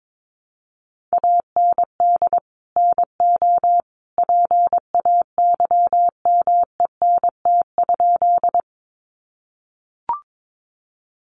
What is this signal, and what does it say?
Morse code, heard as a keyed tone, about 22 wpm, 700 Hz. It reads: AND NO PAYMENT?
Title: Sentences from Top 1200 Words 22wpm